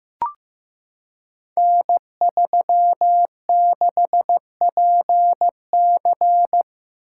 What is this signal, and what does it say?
Morse code, heard as a keyed tone, about 15 wpm, 700 Hz. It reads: N36PC
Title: Code Groups - 5 Letter:Numbers - 15wpm